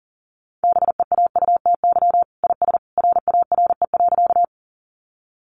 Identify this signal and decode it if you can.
Morse code, heard as a keyed tone, about 40 wpm, 700 Hz. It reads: BEAUTY IS RARE.
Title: Sentences from Top 1100 Words 40wpm